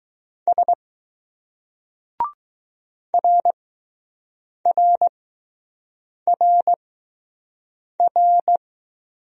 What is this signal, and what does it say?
Morse code, heard as a keyed tone, about 20 wpm, spaced as for 6 wpm, 700 Hz. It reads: S RRRR